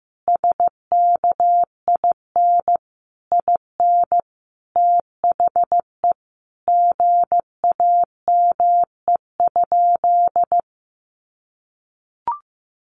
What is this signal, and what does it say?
Morse code, heard as a keyed tone, about 15 wpm, 700 Hz. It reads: SKIN IN THE GAME?
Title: Sentences from Top 1300 Words 15wpm